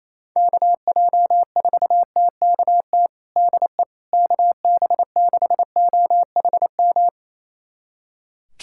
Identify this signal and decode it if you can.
Morse code, heard as a keyed tone, about 28 wpm, 700 Hz. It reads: KJ4TKT DE KB6OHM
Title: USA Callsign DE Callsign 28wpm